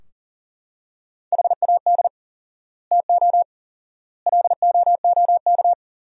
Morse code, heard as a keyed tone, about 37 wpm, 700 Hz. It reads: HAD TO LOOK
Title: 4x Word-Spacing - Sentences from Top 100 Words - 40wpm